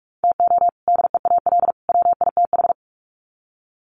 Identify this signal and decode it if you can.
Morse code, heard as a keyed tone, about 44 wpm, 700 Hz. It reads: TO DEAL WITH
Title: Sets of 3 Words 45wpm